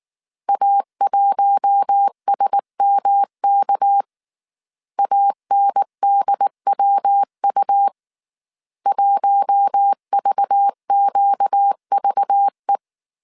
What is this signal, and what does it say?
Morse code, heard as a keyed tone, about 19 wpm, 800 Hz. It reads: A1SMK ANDWU 1VQVE
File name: CW / DET / 1 6 1500C S A 18wpm 2255